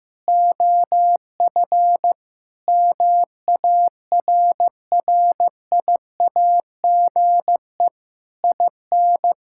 Morse code, heard as a keyed tone, about 15 wpm, 700 Hz. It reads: OF MARRIAGE IN
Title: Sentences from Top 1000 Words - Review Part 5 - 15wpm